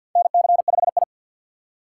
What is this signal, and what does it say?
Morse code, heard as a keyed tone, about 53 wpm, 700 Hz. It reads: NC5I
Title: USA Call Signs - 4 Characters - 3x Character Spacing - 50wpm